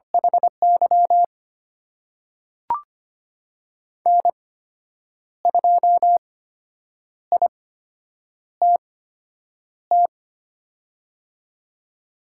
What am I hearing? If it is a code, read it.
Morse code, heard as a keyed tone, about 25 wpm, spaced as for 8 wpm, 700 Hz. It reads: HY N2ITT
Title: USA Call Signs - 8x Character Spacing - 25wpm